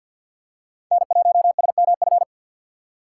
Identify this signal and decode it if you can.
Morse code, heard as a keyed tone, about 47 wpm, 700 Hz. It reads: N1SMF
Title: USA Call Signs - 5 Characters - 2x Character Spacing - 50wpm